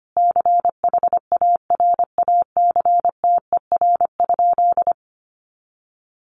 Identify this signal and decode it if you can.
Morse code, heard as a keyed tone, about 25 wpm, 700 Hz. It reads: CHARACTER?
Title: Sentences from Top 1200 Words 25wpm